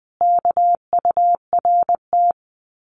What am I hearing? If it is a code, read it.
Morse code, heard as a keyed tone, about 20 wpm, 700 Hz. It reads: KURT
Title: Common QSO Phrases Encore 20wpm